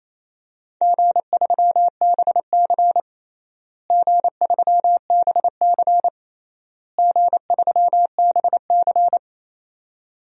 Morse code, heard as a keyed tone, about 28 wpm, 700 Hz. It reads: G3BC G3BC G3BC
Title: DXCC-100 Call Signs - 4 Characters - Repeated 3x - 28wpm